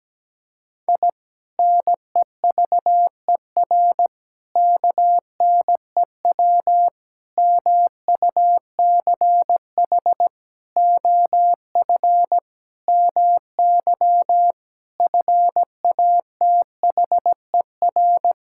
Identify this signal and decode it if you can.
Morse code, heard as a keyed tone, about 17 wpm, 700 Hz. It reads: I NEVER KNEW MUCH OF MY FATHER
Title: Sentences from Top 1000 Words - Review Part 6 - 17wpm